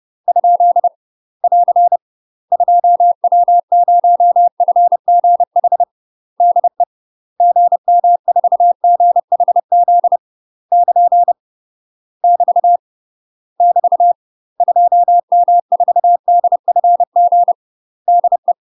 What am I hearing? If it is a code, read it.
Morse code, heard as a keyed tone, about 30 wpm, 700 Hz. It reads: ? <AR> 2W0FGH DE GM4GHZ <KN> <BT> <BT> 2M4DFG DE
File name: UK MASTER QSOs TRAINING / 30WPM / UK MASTER QSOs 0045